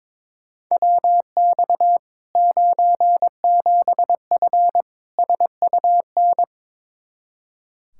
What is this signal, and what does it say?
Morse code, heard as a keyed tone, about 22 wpm, 700 Hz. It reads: WX 97F SUN
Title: Practice QSO - Exchange 2 - Info - 22wpm